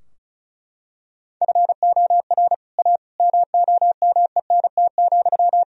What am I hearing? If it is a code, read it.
Morse code, heard as a keyed tone, about 36 wpm, 700 Hz. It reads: FOR A MOMENT,
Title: Sentences from Top 1000 Words - Review Part 2 - 35wpm